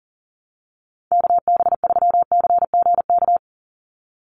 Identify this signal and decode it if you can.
Morse code, heard as a keyed tone, about 40 wpm, 700 Hz. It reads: KB3CGK
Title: USA Call Signs 40wpm